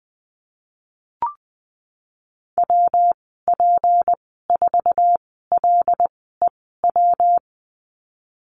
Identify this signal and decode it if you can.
Morse code, heard as a keyed tone, about 20 wpm, spaced as for 14 wpm, 700 Hz. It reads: WP4LEW
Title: 173 Lesson 28g - Number 9 - Call Signs 20wpm